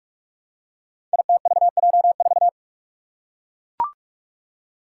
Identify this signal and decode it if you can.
Morse code, heard as a keyed tone, about 44 wpm, 700 Hz. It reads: ITVJ4
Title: Five Challenging Letter-Number Combinations 45wpm